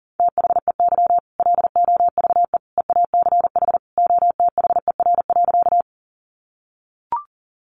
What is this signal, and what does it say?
Morse code, heard as a keyed tone, about 40 wpm, 700 Hz. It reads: THEY LOVE EACH OTHER.
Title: Easy Phrases 40wpm